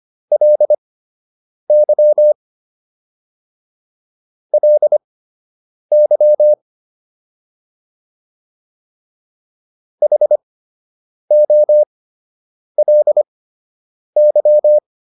Morse code, heard as a keyed tone, about 25 wpm, spaced as for 8 wpm, 600 Hz. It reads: LY LY HOLY